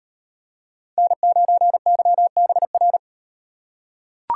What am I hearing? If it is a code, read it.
Morse code, heard as a keyed tone, about 38 wpm, 700 Hz. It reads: N9YBR